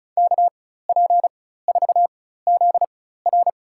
Morse code, heard as a keyed tone, about 34 wpm, spaced as for 16 wpm, 700 Hz. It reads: KP4ZR
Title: USA Call Signs - 5 Characters - 4x Character Spacing - 35wpm